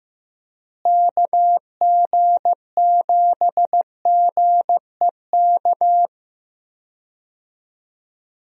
Morse code, heard as a keyed tone, about 15 wpm, 700 Hz. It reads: KG7GEK
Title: USA Call Signs - 6 Characters - Repeated 2x - 15wpm